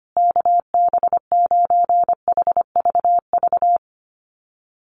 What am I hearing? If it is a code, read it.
Morse code, heard as a keyed tone, about 25 wpm, 700 Hz. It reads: KB9HVV
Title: USA Call Signs Repeated 3x 25wpm